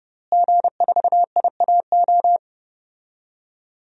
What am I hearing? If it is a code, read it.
Morse code, heard as a keyed tone, about 32 wpm, 700 Hz. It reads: G4IAO
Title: DXCC-100 Call Signs - 5 Characters - Repeated 2x - 30wpm